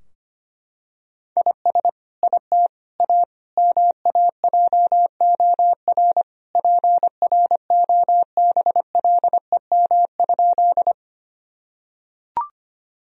Morse code, heard as a keyed tone, about 25 wpm, 700 Hz. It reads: IS IT A MAJOR PROBLEM?
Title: Sentences from Top 1000 Words - Review Part 4 - 25wpm